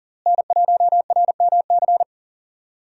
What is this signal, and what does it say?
Morse code, heard as a keyed tone, about 40 wpm, 700 Hz. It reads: N1RMC
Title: USA Call Signs - 5 Characters - Repeated 3x - 40wpm